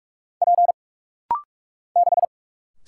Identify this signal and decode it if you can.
Morse code, heard as a keyed tone, about 47 wpm, 700 Hz. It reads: P 6
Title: Single Letter-Number - Rapid-Fire 45wpm